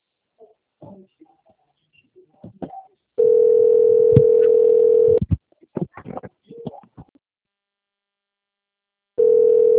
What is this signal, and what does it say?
Ringback tone (North America), 440 + 480 Hz.